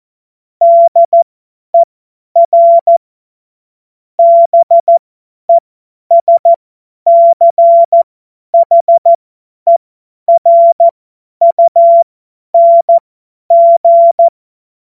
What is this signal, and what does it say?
Morse code, heard as a keyed tone, about 14 wpm, spaced as for 10 wpm, 680 Hz. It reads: DER BESCHERUNG
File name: Morsekurs / Witze / Witz 03. Januar 2023 50 F70